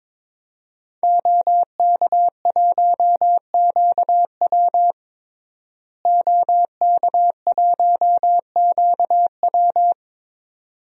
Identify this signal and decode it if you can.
Morse code, heard as a keyed tone, about 22 wpm, 700 Hz. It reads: OK1QW OK1QW
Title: DXCC-100 Call Signs - 5 Characters - Repeated 2x - 22wpm